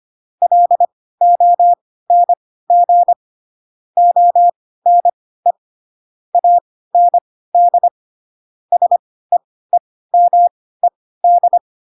Morse code, heard as a keyed tone, about 25 wpm, spaced as for 16 wpm, 700 Hz. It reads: LONG ONE AND SEEMED
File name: Pinocchio / Pinocchio ch3 10 25 ik1hge